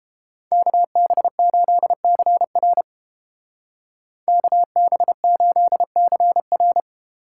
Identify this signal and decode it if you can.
Morse code, heard as a keyed tone, about 31 wpm, 700 Hz. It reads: KB8CR KB8CR